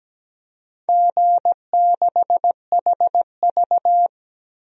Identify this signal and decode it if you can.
Morse code, heard as a keyed tone, about 17 wpm, 700 Hz. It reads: G6HV
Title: DXCC-100 Call Signs - 4 Characters - Repeated 3x - 17wpm